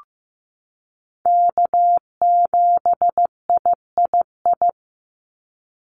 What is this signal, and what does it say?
Morse code, heard as a keyed tone, about 15 wpm, 700 Hz. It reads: K7III
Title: NA SOTA Chaser Callsigns 15wpm